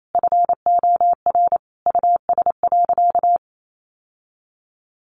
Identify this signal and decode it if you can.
Morse code, heard as a keyed tone, about 28 wpm, 700 Hz. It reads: FOR US.
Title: Sentences from Top 1200 Words 28wpm